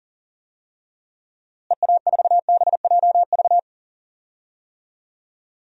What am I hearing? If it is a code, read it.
Morse code, heard as a keyed tone, about 38 wpm, 700 Hz. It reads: EA4BJV
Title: DXCC-100 Call Signs - 6 Characters - Repeated 3x - 40wpm